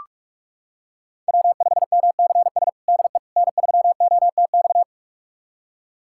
Morse code, heard as a keyed tone, about 46 wpm, 700 Hz. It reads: W5MCS DE N3OTX
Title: USA Callsign DE Callsign - 5 Characters - 45wpm